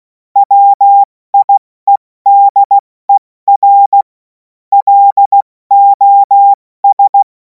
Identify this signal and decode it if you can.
Morse code, heard as a keyed tone, about 16 wpm, spaced as for 14 wpm, 800 Hz. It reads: WIEDER LOS
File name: Morsekurs / Witze / Witz 28. April 2026 70 F80